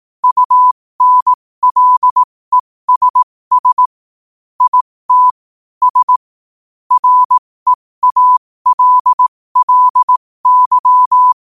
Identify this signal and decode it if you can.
Morse code, heard as a keyed tone, about 18 wpm, spaced as for 15 wpm, 1000 Hz. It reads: UNLESS IT S REALLY
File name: qotdfiles / 15wpm